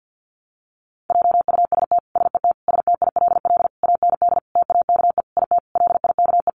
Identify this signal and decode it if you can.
Morse code, heard as a keyed tone, about 50 wpm, 700 Hz. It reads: JUST SET STILL AND TAKE IT LIKE
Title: Sentences from Top 300 Words 50wpm